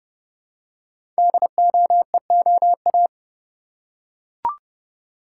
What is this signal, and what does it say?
Morse code, heard as a keyed tone, about 30 wpm, 700 Hz. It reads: DOEOA
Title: Code Groups - 5 Letters - 30wpm